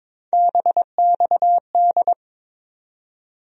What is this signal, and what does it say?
Morse code, heard as a keyed tone, about 21 wpm, 700 Hz. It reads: BXD
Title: USA Call Sign Postfixes - 3 Letters 22wpm